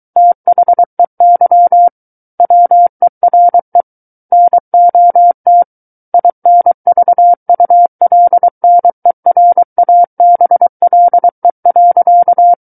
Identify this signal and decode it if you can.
Morse code, heard as a keyed tone, about 23 wpm, 700 Hz. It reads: THEY WERE NOT INVULNERABLE.
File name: Book One / 14-Chapter 14 - 23 WPM